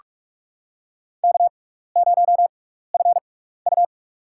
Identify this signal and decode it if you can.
Morse code, heard as a keyed tone, about 45 wpm, spaced as for 15 wpm, 700 Hz. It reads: K0FU
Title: USA Call Signs - 6x Character Spacing - 45wpm